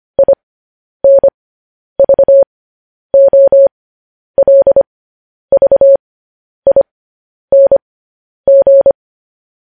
Morse code, heard as a keyed tone, about 25 wpm, spaced as for 10 wpm, 550 Hz. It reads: INVOLVING
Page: News Headlines in Morse Code at 10 WPM